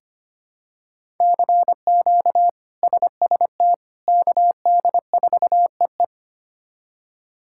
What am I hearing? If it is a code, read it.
Morse code, heard as a keyed tone, about 25 wpm, 700 Hz. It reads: CQ SST KD4EE
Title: K1USN SST QSO Practice - 25wpm